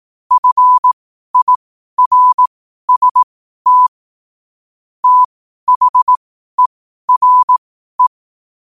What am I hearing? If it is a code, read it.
Morse code, heard as a keyed tone, about 18 wpm, spaced as for 12 wpm, 1000 Hz. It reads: FIRST THERE
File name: qotdfiles / 12wpm